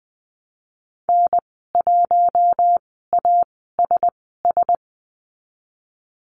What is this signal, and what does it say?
Morse code, heard as a keyed tone, about 20 wpm, spaced as for 15 wpm, 700 Hz. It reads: N1ASS
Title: 076 Lesson 13g - Letter L - Call Signs 20wpm